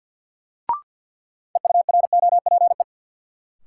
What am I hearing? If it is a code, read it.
Morse code, heard as a keyed tone, about 52 wpm, 700 Hz. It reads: EUROPE
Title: Sets of 5 Words Encore - New Word Drill - 50wpm